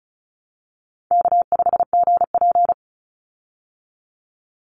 Morse code, heard as a keyed tone, about 34 wpm, 700 Hz. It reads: K5GP
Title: Easy Call Signs Repeated 3x - 35wpm